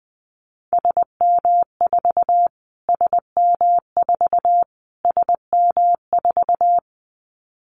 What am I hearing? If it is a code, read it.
Morse code, heard as a keyed tone, about 20 wpm, 700 Hz. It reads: SM4 SM4 SM4
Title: Callsign Prefixes for Top 25 DXCC - Repeated 3x - 20wpm